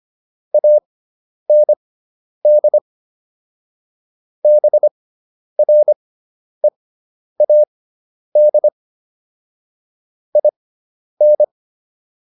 Morse code, heard as a keyed tone, about 25 wpm, spaced as for 10 wpm, 600 Hz. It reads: AND BREAD IN